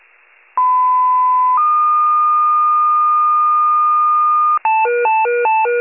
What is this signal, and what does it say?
Two-tone page, heard as a one-second tone, then a three-second tone, about 1010 then 1220 Hz. It is heard from 0.6 to 4.6 s.